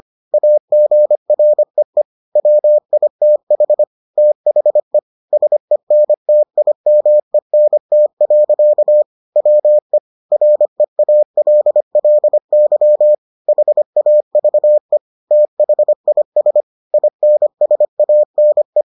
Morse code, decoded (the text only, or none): AGREE WITH THE SENTIMENT. WE REALLY HAVE THIS INSANE